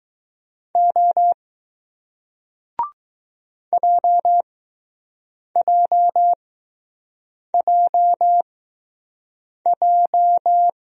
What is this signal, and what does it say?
Morse code, heard as a keyed tone, about 20 wpm, spaced as for 6 wpm, 700 Hz. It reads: O JJJJ